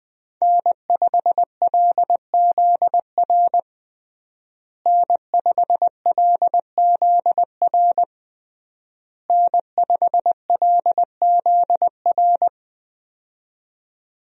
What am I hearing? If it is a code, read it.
Morse code, heard as a keyed tone, about 20 wpm, 700 Hz. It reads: N5LZR N5LZR N5LZR
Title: USA Call Signs - 5 Characters - Repeated 3x - 20wpm